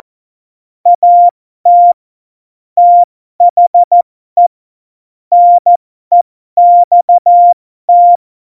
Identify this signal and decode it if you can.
Morse code, heard as a keyed tone, about 14 wpm, spaced as for 12 wpm, 700 Hz. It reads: AT THE NEXT
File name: Morsekurs / Witze